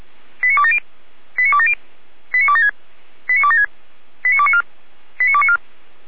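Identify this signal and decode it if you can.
Five-tone selective call (ZVEI), heard as quick runs of five tones, tones of about 70 ms. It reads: 89270 89270 89276 89276 89294 89294
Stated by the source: ZVEI five-tone selective call received on an SDR by the fire brigade (Feuerwehr) Orlamünde.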